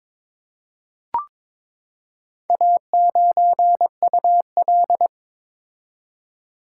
Morse code, heard as a keyed tone, about 22 wpm, 690 Hz. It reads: A9UL
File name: practice / 22wpm / Four Letter-Number Combinations - 22wpm